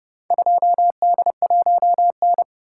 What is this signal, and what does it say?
Morse code, heard as a keyed tone, about 28 wpm, 700 Hz. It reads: 2D1N